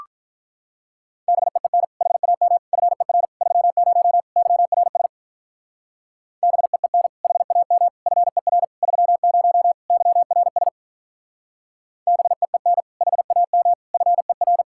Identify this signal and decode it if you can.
Morse code, heard as a keyed tone, about 44 wpm, 700 Hz. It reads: BEEN HAM FER 30 YRS BEEN HAM FER 30 YRS BEEN HAM FER